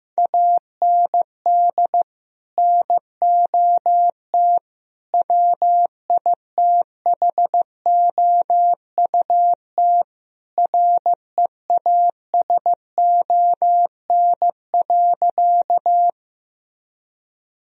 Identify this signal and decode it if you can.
Morse code, heard as a keyed tone, about 16 wpm, 700 Hz. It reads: AND NOT WITHOUT REASON.
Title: Sentences from Top 1000 Words - Review Part 3 - 15wpm